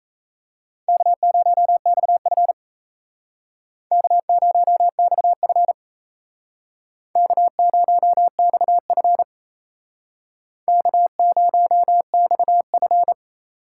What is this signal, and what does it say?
Morse code, heard as a keyed tone, about 37 wpm, 700 Hz. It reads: K0XF K0XF K0XF K0XF